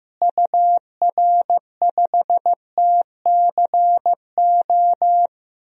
Morse code, heard as a keyed tone, about 15 wpm, 700 Hz. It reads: UR5TCO
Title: DXCC-100 Call Signs - 6 Characters - Repeated 2x - 15wpm